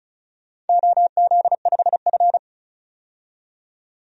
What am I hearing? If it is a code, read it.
Morse code, heard as a keyed tone, about 36 wpm, 700 Hz. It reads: OZ5F
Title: DXCC-100 Call Signs - 4 Characters - 35wpm